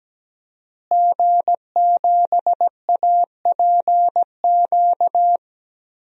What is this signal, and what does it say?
Morse code, heard as a keyed tone, about 17 wpm, 700 Hz. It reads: G7APQ
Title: DXCC-100 Call Signs - 5 Characters - Repeated 3x - 17wpm